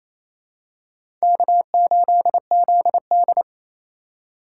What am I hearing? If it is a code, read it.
Morse code, heard as a keyed tone, about 28 wpm, 700 Hz. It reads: K8ZD
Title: USA Call Signs - 4x Character Spacing - 28wpm